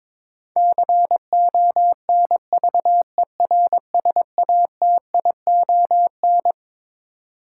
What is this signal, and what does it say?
Morse code, heard as a keyed tone, about 21 wpm, 700 Hz. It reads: CONVERSATION
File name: Twelve Letter Words - 22wpm